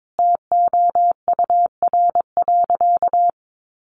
Morse code, heard as a keyed tone, about 22 wpm, 700 Hz. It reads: TOUR.
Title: Sentences from Top 1400 Words 22wpm